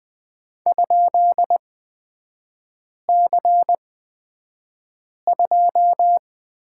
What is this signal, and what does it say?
Morse code, heard as a keyed tone, about 20 wpm, spaced as for 5 wpm, 700 Hz. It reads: ?C2
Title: 243 Lesson 39e - Letter z - Three Characters 20wpm